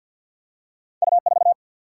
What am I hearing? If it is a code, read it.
Morse code, heard as a keyed tone, about 48 wpm, 700 Hz. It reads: U 4
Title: Single-Letter Single-Number 50wpm